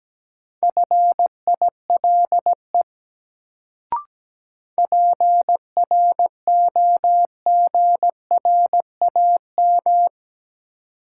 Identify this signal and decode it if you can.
Morse code, heard as a keyed tone, about 17 wpm, 700 Hz. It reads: FILE PROGRAM
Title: Sentences from Top 1000 Words - Review Part 4 - New Word Drill - 17wpm